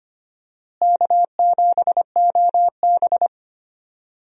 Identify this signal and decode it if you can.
Morse code, heard as a keyed tone, about 25 wpm, 700 Hz. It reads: K7OB
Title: USA Call Signs - 4 Characters - Repeated 2x - 25wpm